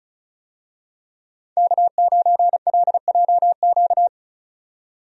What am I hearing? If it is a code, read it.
Morse code, heard as a keyed tone, about 36 wpm, 700 Hz. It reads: K9LJQ
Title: USA Call Signs - 5 Characters - 35wpm